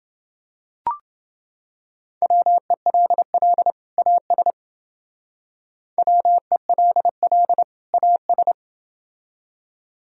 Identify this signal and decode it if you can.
Morse code, heard as a keyed tone, about 28 wpm, 700 Hz. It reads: WELL AS WELL AS